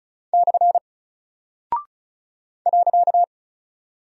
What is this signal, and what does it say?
Morse code, heard as a keyed tone, about 36 wpm, 700 Hz. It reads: / .
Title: Single Character (Letter:Number:Punctuation:Prosign) 35wpm